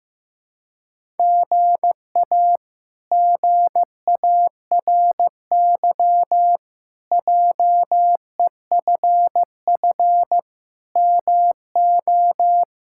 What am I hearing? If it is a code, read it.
Morse code, heard as a keyed tone, about 16 wpm, 700 Hz. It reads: GA GARY JEFF MO
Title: K1USN SST QSO Practice - 15wpm